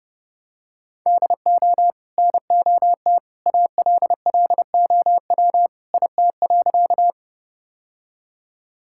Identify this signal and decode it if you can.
Morse code, heard as a keyed tone, about 30 wpm, 700 Hz. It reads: DO NOT ALLOW IT.
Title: Sentences from Top 1000 Words - Review Part 5 - 30wpm